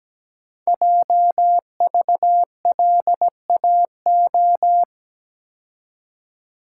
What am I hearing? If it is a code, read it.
Morse code, heard as a keyed tone, about 17 wpm, 700 Hz. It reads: JVLAO
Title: Code Groups - 5 Letter:Numbers - 17wpm